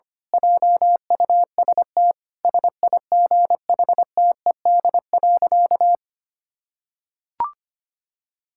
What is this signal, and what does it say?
Morse code, heard as a keyed tone, about 25 wpm, 700 Hz. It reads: JUST SIGHTED.
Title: Pangram Challenge v2 - 25wpm